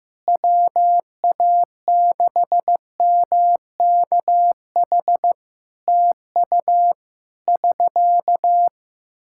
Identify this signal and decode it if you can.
Morse code, heard as a keyed tone, about 15 wpm, 700 Hz. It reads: WA6MKH TU <SK>
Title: K1USN SST QSO Practice 15wpm